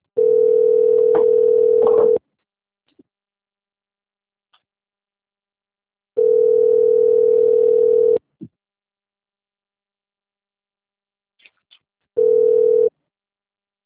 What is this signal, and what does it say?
Ringback tone (North America), 440 + 480 Hz.
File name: out- - - - - .46787